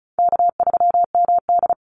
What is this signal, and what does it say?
Morse code, heard as a keyed tone, about 34 wpm, 700 Hz. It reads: K3MD
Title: Call Signs Repeated 3x - 35wpm